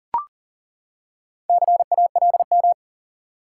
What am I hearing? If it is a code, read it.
Morse code, heard as a keyed tone, about 40 wpm, 700 Hz. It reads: CALM
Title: Four Letter Words - 40wpm